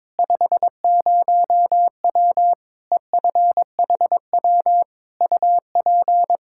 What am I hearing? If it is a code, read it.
Morse code, heard as a keyed tone, about 22 wpm, 700 Hz. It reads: 50W EFHW UP
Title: Practice QSO - Exchange 2 - Info - 22wpm